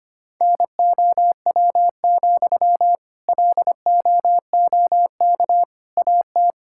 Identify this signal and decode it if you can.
Morse code, heard as a keyed tone, about 25 wpm, 700 Hz. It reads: NOW, LOOK AT
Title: Sentences from Top 1600 Words 25wpm